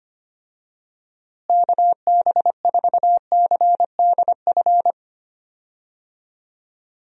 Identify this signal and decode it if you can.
Morse code, heard as a keyed tone, about 25 wpm, 700 Hz. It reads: KB4CDF
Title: USA Call Signs - 6 Characters - Repeated 2x - 25wpm